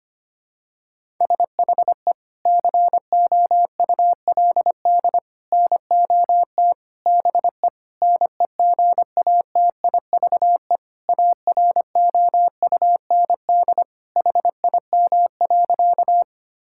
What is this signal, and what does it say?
Morse code, heard as a keyed tone, about 25 wpm, 700 Hz. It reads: SHE COULD NOT BE NEGATIVE AROUND HIM.
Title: Sentences from Top 1000 Words - Review Part 6 - 25wpm